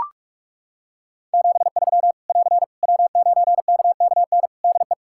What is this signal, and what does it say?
Morse code, heard as a keyed tone, about 41 wpm, 700 Hz. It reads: 73 <AR> W9KKN DE
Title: Practice QSO - Exchange 4 - Ending - 45wpm